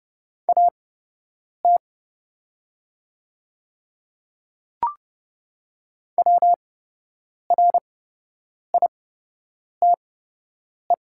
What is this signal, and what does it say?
Morse code, heard as a keyed tone, about 30 wpm, spaced as for 8 wpm, 700 Hz. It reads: AT WRITE
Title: 8x Character Spacing - Top 500 Words - 30wpm